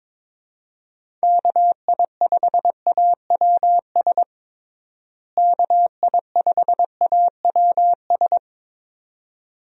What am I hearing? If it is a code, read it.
Morse code, heard as a keyed tone, about 22 wpm, 700 Hz. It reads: KI5AWS KI5AWS
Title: USA Call Signs - 6 Characters - Repeated 2x - 22wpm